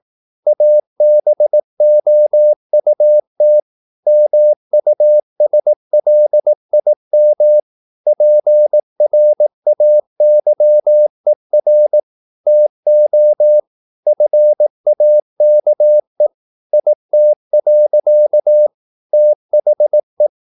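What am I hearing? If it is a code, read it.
Morse code, heard as a keyed tone, about 18 wpm, 600 Hz. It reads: ABOUT MUSLIM PRAYER TO FAKE IT. THE
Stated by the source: Morse practice at 18 wpm, English text, from CW med Gnister.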